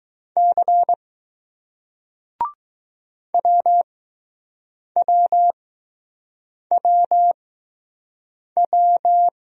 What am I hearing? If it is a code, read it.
Morse code, heard as a keyed tone, about 20 wpm, spaced as for 6 wpm, 700 Hz. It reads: C WWWW